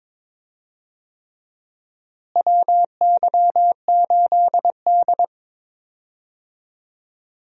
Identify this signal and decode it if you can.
Morse code, heard as a keyed tone, about 22 wpm, 700 Hz. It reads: WY8D
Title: USA Call Signs - 4 Characters - Repeated 3x - 22wpm